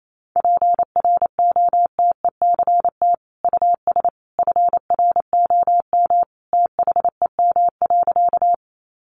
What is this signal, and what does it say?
Morse code, heard as a keyed tone, about 28 wpm, 700 Hz. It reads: PROTECT US FROM THEM.
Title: Sentences from Top 900 Words 28wpm